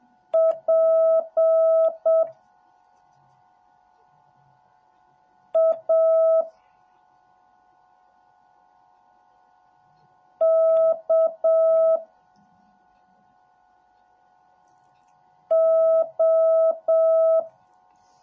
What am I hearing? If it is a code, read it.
Morse code, heard as a keyed tone, about 7 wpm, 650 Hz. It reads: P A K O